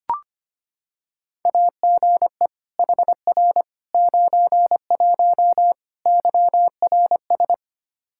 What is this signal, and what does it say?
Morse code, heard as a keyed tone, about 25 wpm, 700 Hz. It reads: AGE HR 91 YRS
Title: Practice QSO - Exchange 3 - More Info - 25wpm